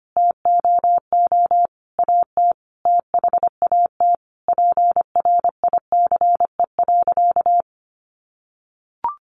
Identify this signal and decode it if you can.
Morse code, heard as a keyed tone, about 25 wpm, 700 Hz. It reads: TOO AT THAT PRICE.
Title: Sentences from Top 1200 Words 25wpm